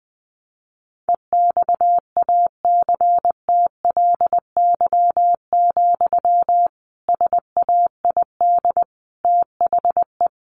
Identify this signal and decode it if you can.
Morse code, heard as a keyed tone, about 20 wpm, 700 Hz. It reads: EXACTLY, SAID THE
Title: Sentences from Top 1500 Words 20wpm